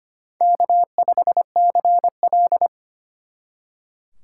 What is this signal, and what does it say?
Morse code, heard as a keyed tone, about 32 wpm, 700 Hz. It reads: K5CL